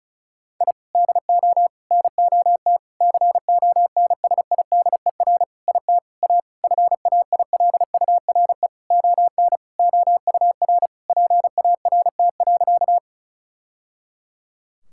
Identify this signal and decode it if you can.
Morse code, heard as a keyed tone, about 34 wpm, 700 Hz. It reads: I DO NOT CONSIDER IT A FAILURE ON OUR PART.
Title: Sentences from Top 1000 Words - Review Part 6 - 35wpm